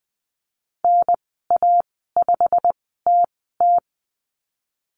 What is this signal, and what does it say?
Morse code, heard as a keyed tone, about 20 wpm, spaced as for 15 wpm, 700 Hz. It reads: NA5TT
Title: 088 Lesson 15g - Number 5 - Call Signs 20wpm